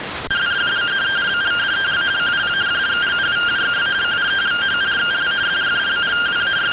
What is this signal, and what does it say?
Two-tone teleprinter signal, about 170 Hz shift.